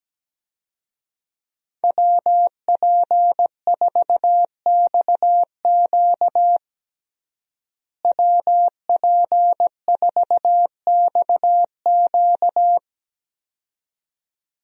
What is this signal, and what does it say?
Morse code, heard as a keyed tone, about 17 wpm, 700 Hz. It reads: WP4XQ WP4XQ
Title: USA Call Signs - 5 Characters - Repeated 2x - 17wpm